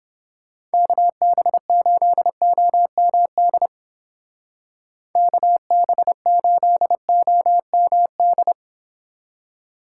Morse code, heard as a keyed tone, about 30 wpm, 700 Hz. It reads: KB8OMD KB8OMD